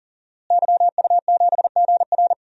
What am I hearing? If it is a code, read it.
Morse code, heard as a keyed tone, about 39 wpm, 700 Hz. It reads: YU7GR
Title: DXCC-100 Call Signs - 5 Characters - Repeated 2x - 40wpm